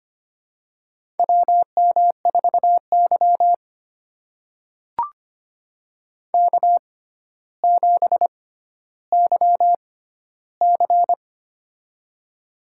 Morse code, heard as a keyed tone, about 25 wpm, spaced as for 13 wpm, 700 Hz. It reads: WM4Y K7YC
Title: USA Call Signs - 4 Characters - 6x Character Spacing - 25wpm